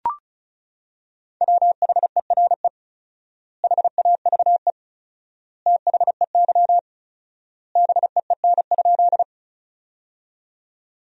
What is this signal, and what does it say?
Morse code, heard as a keyed tone, about 34 wpm, 700 Hz. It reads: WHERE HAVE THEY BEEN?
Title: 4x Word-Spacing - Sentences from Top 200 Words - 35wpm